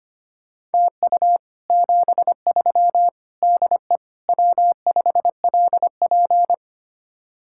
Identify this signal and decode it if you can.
Morse code, heard as a keyed tone, about 32 wpm, 700 Hz. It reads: TU 73 DE W5LP